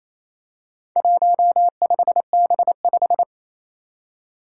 Morse code, heard as a keyed tone, about 27 wpm, 700 Hz. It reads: 15B5
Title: Four Challenging Letter-Number Combinations 28wpm